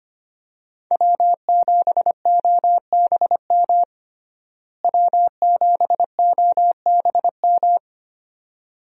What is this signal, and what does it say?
Morse code, heard as a keyed tone, about 25 wpm, 700 Hz. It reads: W7OBM W7OBM
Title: USA Call Signs - 5 Characters - Repeated 2x - 25wpm